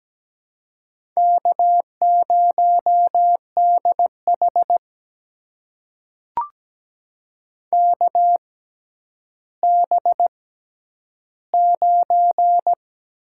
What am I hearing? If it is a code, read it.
Morse code, heard as a keyed tone, about 17 wpm, spaced as for 10 wpm, 700 Hz. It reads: K0DH KB9
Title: USA Call Signs - 6x Character Spacing - 17wpm